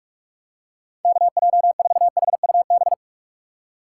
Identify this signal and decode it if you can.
Morse code, heard as a keyed tone, about 46 wpm, 700 Hz. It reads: KJ4HUB
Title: USA Call Signs - 6 Characters - 3x Character Spacing - 45wpm